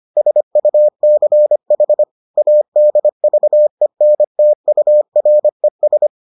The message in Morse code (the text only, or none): SUCH ADVENTURES